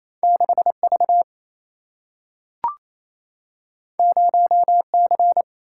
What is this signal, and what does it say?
Morse code, heard as a keyed tone, about 28 wpm, 700 Hz. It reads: 6 V 0 C
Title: Single-Letter Single-Number 28wpm